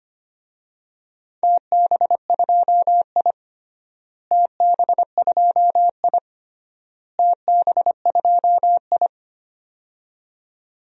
Morse code, heard as a keyed tone, about 25 wpm, 700 Hz. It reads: TB2I TB2I TB2I
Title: DXCC-100 Call Signs - 4 Characters - Repeated 3x - 25wpm